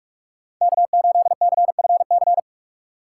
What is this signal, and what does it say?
Morse code, heard as a keyed tone, about 46 wpm, 700 Hz. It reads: K8CFC
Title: USA Call Signs - 5 Characters - 3x Character Spacing - 45wpm